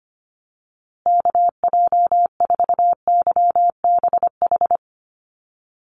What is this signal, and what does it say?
Morse code, heard as a keyed tone, about 25 wpm, 700 Hz. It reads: KJ4YBH
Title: USA Call Signs Repeated 2x 25wpm